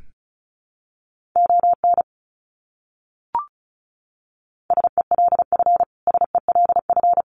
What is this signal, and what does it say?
Morse code, heard as a keyed tone, about 34 wpm, 700 Hz. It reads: ON SELF SELF
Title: Top 300 Words Repeated 35wpm